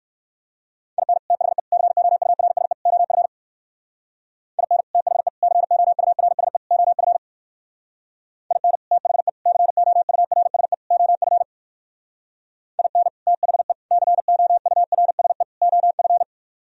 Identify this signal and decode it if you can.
Morse code, heard as a keyed tone, about 64 wpm, 700 Hz. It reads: IN THE COURSE OF IN THE COURSE OF IN THE COURSE OF IN THE COURSE OF